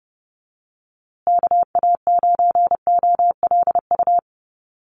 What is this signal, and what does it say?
Morse code, heard as a keyed tone, about 30 wpm, 700 Hz. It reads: KA9OLU
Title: USA Call Signs Repeated 2x 30wpm